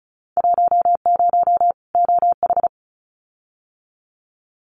Morse code, heard as a keyed tone, about 34 wpm, 700 Hz. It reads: 10 OH
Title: Winter Field Day Exchanges 35wpm